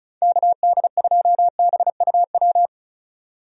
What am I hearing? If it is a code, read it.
Morse code, heard as a keyed tone, about 36 wpm, 700 Hz. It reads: KD2BUW
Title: USA Call Signs - 6 Characters - 35wpm